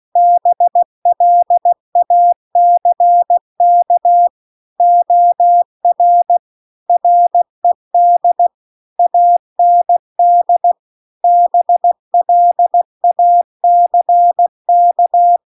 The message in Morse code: BLACK OR RED AND BLACK

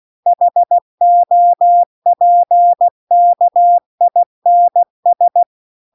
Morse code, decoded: HOPKINS

